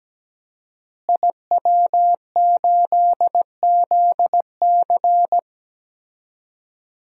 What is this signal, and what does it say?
Morse code, heard as a keyed tone, about 17 wpm, 700 Hz. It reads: IW8ZC
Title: DXCC-100 Call Signs - 5 Characters - Repeated 3x - 17wpm